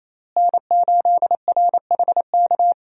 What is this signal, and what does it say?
Morse code, heard as a keyed tone, about 28 wpm, 700 Hz. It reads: N8RHK
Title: USA Call Signs - 5 Characters - Repeated 3x - 28wpm